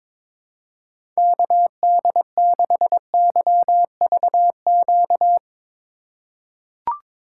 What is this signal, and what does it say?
Morse code, heard as a keyed tone, about 22 wpm, 700 Hz. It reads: KD6YVQ
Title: USA Call Signs - 6 Characters - 3x Character Spacing - 22wpm